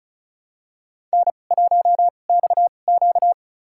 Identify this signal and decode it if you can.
Morse code, heard as a keyed tone, about 34 wpm, spaced as for 25 wpm, 690 Hz. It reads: N1XQ
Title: USA Call Signs - 4 Characters - 2x Character Spacing - 35wpm